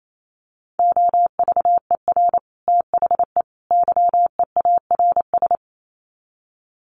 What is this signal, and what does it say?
Morse code, heard as a keyed tone, about 28 wpm, 700 Hz. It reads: OVER THE YEARS
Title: Sets of 3 Words 28wpm